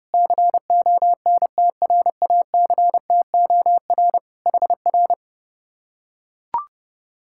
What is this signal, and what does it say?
Morse code, heard as a keyed tone, about 30 wpm, 700 Hz. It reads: CONTRACTOR HR